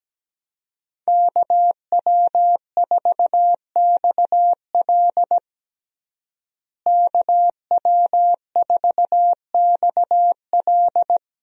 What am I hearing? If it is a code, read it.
Morse code, heard as a keyed tone, about 17 wpm, 700 Hz. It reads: KW4XL KW4XL
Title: USA Call Signs - 5 Characters - Repeated 3x - 17wpm